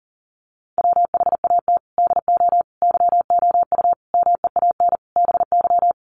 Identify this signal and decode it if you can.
Morse code, heard as a keyed tone, about 40 wpm, 700 Hz. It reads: WHAT DO YOU MEAN BY
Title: Sentences from Top 1300 Words 40wpm